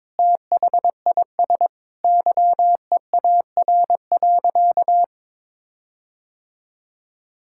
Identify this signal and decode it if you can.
Morse code, heard as a keyed tone, about 22 wpm, 700 Hz. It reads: THIS YEAR.
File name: practice / 22wpm / Sentences with 8-Word Phrases - 22wpm